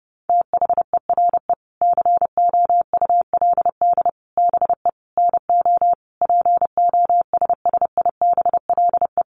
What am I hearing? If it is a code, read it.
Morse code, heard as a keyed tone, about 30 wpm, 700 Hz. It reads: THERE COULD BE NO POSSIBLE
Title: Sentences from Top 700 Words 30wpm